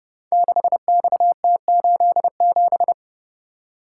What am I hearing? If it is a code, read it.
Morse code, heard as a keyed tone, about 29 wpm, 700 Hz. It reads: 6XT87
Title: Code Groups - 5 Letter:Numbers - 30wpm